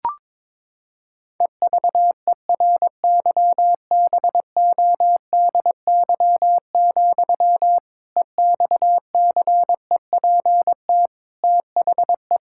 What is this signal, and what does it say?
Morse code, heard as a keyed tone, about 22 wpm, 700 Hz. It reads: EVERYBODY, EXCEPT THE
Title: Sentences from Top 1600 Words 22wpm